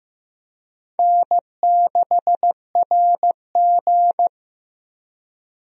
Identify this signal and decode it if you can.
Morse code, heard as a keyed tone, about 15 wpm, 700 Hz. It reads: N6RG
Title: USA Call Signs - 4 Characters - Repeated 2x - 15wpm